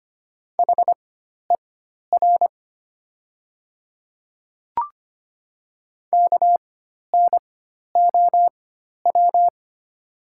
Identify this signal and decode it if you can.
Morse code, heard as a keyed tone, about 25 wpm, spaced as for 12 wpm, 700 Hz. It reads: HER KNOW
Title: 4x Character Spacing - Top 300 Words - 25wpm